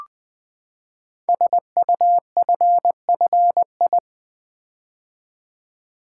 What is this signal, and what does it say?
Morse code, heard as a keyed tone, about 20 wpm, 700 Hz. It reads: SUFFI
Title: Build A Word - 5 Letters At A Time - 12-Letter Words - 20wpm